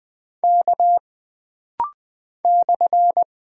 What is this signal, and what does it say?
Morse code, heard as a keyed tone, about 20 wpm, 700 Hz. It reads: K /
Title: Single Character (Letter:Number:Punctuation:Prosign) - Rapid-Fire - 20wpm